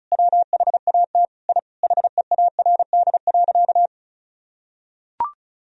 Morse code, heard as a keyed tone, about 36 wpm, 700 Hz. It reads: WHAT I HEARD.
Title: Sentences from Top 1000 Words - Review Part 4 - 35wpm